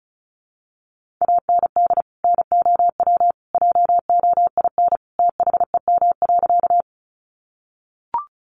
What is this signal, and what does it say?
Morse code, heard as a keyed tone, about 34 wpm, 700 Hz. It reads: AND NOW JOIN THEM.
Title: Sentences from Top 1400 Words 35wpm